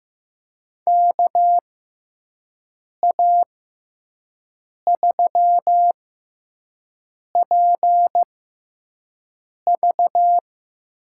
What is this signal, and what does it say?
Morse code, heard as a keyed tone, about 15 wpm, spaced as for 5 wpm, 700 Hz. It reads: KA3PV
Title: USA Call Signs - 6 Characters - 6x Character Spacing - 15wpm